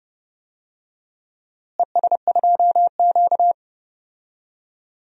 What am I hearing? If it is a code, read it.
Morse code, heard as a keyed tone, about 31 wpm, 700 Hz. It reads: ES2Q
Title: DXCC-100 Call Signs - 4 Characters - Repeated 2x - 30wpm